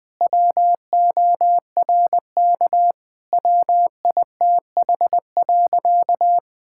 Morse code, heard as a keyed tone, about 21 wpm, 700 Hz. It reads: WORK WITH.
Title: Sentences from Top 1000 Words - Review Part 5 - 20wpm